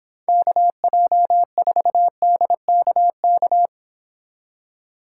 Morse code, heard as a keyed tone, about 29 wpm, 700 Hz. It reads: KJ4DKK